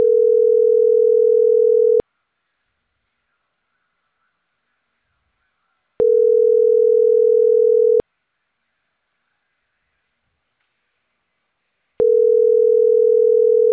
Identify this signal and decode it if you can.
Ringback tone (North America), 440 + 480 Hz.